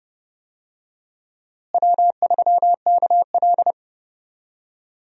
Morse code, heard as a keyed tone, about 30 wpm, 700 Hz. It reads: W3KL
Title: USA Call Signs - 4 Characters - Repeated 3x - 30wpm